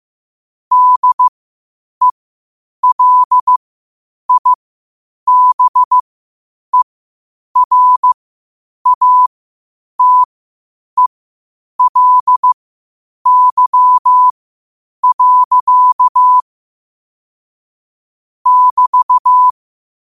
Morse code, heard as a keyed tone, about 15 wpm, spaced as for 8 wpm, 1000 Hz. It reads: DELIBERATELY. <BT>
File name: qotdfiles / 8wpm